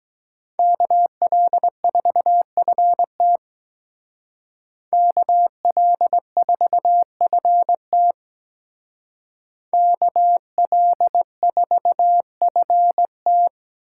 Morse code, heard as a keyed tone, about 20 wpm, 700 Hz. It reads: KL4FT KL4FT KL4FT
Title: Speed Racing - USA Call Signs 17wpm